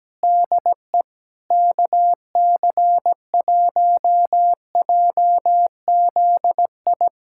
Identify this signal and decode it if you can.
Morse code, heard as a keyed tone, about 17 wpm, 700 Hz. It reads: DE KC1JZI
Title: USA Callsign DE Callsign - 6 Characters - 17wpm